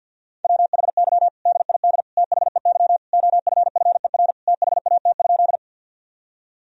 Morse code, heard as a keyed tone, about 50 wpm, 700 Hz. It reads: WHY DID THEY OFFER THAT?
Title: Sentences from Top 1000 Words - Review Part 4 - 50wpm